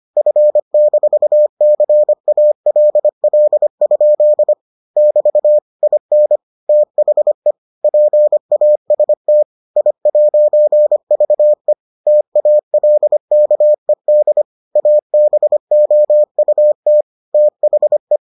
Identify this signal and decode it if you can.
Morse code, heard as a keyed tone, about 25 wpm, 600 Hz. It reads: F-CALL? <BT> IN THE PAST I'VE TALKED ABOUT THE